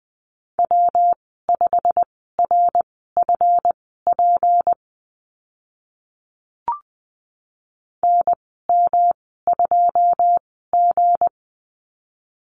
Morse code, heard as a keyed tone, about 20 wpm, spaced as for 14 wpm, 700 Hz. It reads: W5RFP NM2G
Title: 207 Lesson 33g - Letter j - Call Signs 20wpm